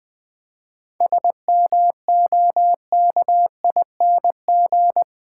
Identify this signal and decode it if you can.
Morse code, heard as a keyed tone, about 20 wpm, 700 Hz. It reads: SMOKING
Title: 7 Letter Words - First 3 Letters Given - 20wpm